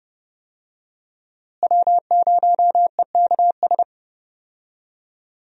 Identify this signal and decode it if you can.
Morse code, heard as a keyed tone, about 31 wpm, 700 Hz. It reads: W0EKS
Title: USA Call Signs - 5 Characters - Repeated 3x - 30wpm